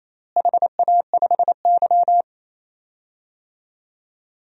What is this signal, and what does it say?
Morse code, heard as a keyed tone, about 27 wpm, 700 Hz. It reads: HA5Y
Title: DXCC-100 Call Signs - 4 Characters - Repeated 3x - 28wpm